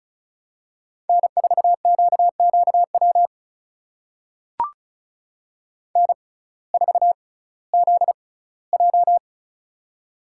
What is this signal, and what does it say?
Morse code, heard as a keyed tone, about 36 wpm, spaced as for 19 wpm, 700 Hz. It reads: N4QQW N4ZJ
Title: USA Call Signs - 6x Character Spacing - 35wpm